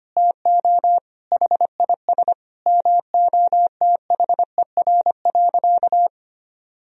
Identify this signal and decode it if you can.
Morse code, heard as a keyed tone, about 25 wpm, 700 Hz. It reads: TO HIS MOTHER.
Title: Sentences from Top 1000 Words - Review Part 3 - 25wpm